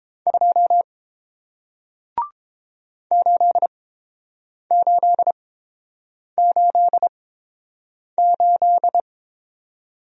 Morse code, heard as a keyed tone, about 30 wpm, spaced as for 7 wpm, 700 Hz. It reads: 2 8888